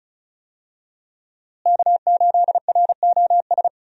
Morse code, heard as a keyed tone, about 34 wpm, 700 Hz. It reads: K8ROS
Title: USA Call Signs - 5 Characters - Repeated 3x - 35wpm